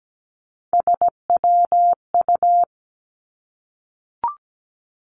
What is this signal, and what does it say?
Morse code, heard as a keyed tone, about 17 wpm, 700 Hz. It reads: SWU
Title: Three Challenging Letter-Number Combinations 17wpm